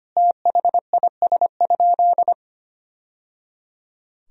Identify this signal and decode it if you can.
Morse code, heard as a keyed tone, about 25 wpm, 700 Hz. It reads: THIS?
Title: Sentences from Top 1000 Words - Review Part 1 - 25wpm